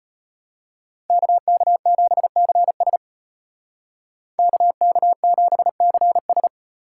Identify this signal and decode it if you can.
Morse code, heard as a keyed tone, about 33 wpm, 700 Hz. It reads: KK7CS KK7CS